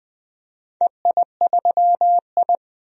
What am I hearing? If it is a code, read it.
Morse code, heard as a keyed tone, about 20 wpm, 700 Hz. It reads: EI3I
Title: DXCC-100 Call Signs - 4 Characters - 20wpm